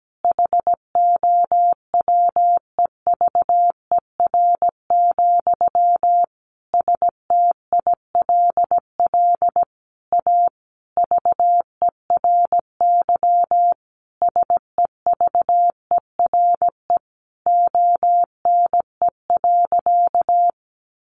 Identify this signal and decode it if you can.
Morse code, heard as a keyed tone, about 17 wpm, 700 Hz. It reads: HOWEVER, STILL A VERY SEVERE ONE.
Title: Sentences from Top 1400 Words 17wpm